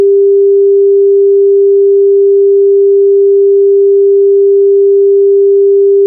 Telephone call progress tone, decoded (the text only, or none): dial tone